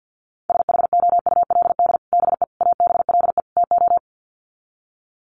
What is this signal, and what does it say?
Morse code, heard as a keyed tone, about 50 wpm, 700 Hz. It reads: SHOULD BE ABLE TO
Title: Sets of 4 Words 50wpm